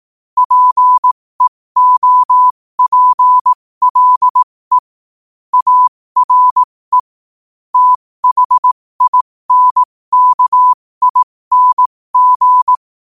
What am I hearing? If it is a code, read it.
Morse code, heard as a keyed tone, about 18 wpm, spaced as for 15 wpm, 1000 Hz. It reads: PEOPLE ARE THINKING
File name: qotdfiles / 15wpm